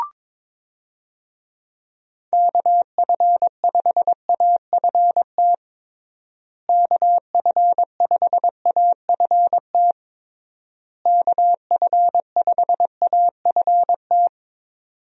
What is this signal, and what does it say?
Morse code, heard as a keyed tone, about 22 wpm, 700 Hz. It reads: KF5AFT KF5AFT KF5AFT
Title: USA Call Signs - 6 Characters - Repeated 3x - 22wpm